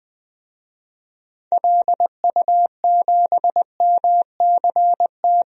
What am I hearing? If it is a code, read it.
Morse code, heard as a keyed tone, about 20 wpm, 700 Hz. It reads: LU7MCT
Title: DXCC-100 Call Signs - 6 Characters - 20wpm